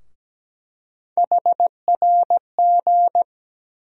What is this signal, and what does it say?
Morse code, heard as a keyed tone, about 17 wpm, 700 Hz. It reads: HRG
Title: USA Call Sign Postfixes - 3 Letters 17wpm